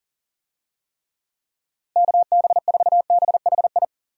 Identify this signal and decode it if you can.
Morse code, heard as a keyed tone, about 40 wpm, 700 Hz. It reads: KB4BHI
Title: USA Call Signs - 6 Characters - Repeated 2x - 40wpm